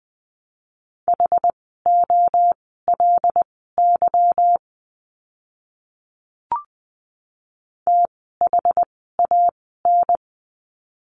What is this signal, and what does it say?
Morse code, heard as a keyed tone, about 19 wpm, spaced as for 14 wpm, 700 Hz. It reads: HOLY THAN
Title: 146 Lesson 24f - Letter y - Words 20wpm